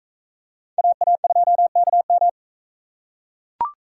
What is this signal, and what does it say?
Morse code, heard as a keyed tone, about 43 wpm, 700 Hz. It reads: AA2KM